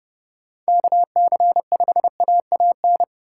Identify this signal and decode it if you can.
Morse code, heard as a keyed tone, about 27 wpm, 700 Hz. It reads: KC5AAN